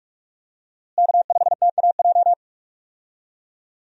Morse code, heard as a keyed tone, about 47 wpm, 700 Hz. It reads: K5TAJ
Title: USA Call Signs - 5 Characters - Repeated 3x - 45wpm